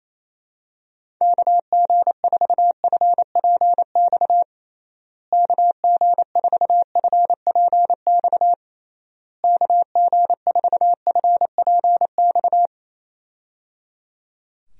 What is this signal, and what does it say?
Morse code, heard as a keyed tone, about 28 wpm, 700 Hz. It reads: KG4FPX KG4FPX KG4FPX
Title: USA Call Signs - 6 Characters - Repeated 3x - 28wpm